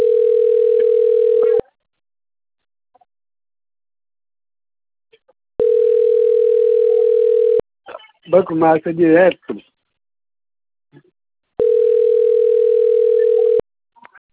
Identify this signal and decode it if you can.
Ringback tone (North America), 440 + 480 Hz, with speech.
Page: Atibaia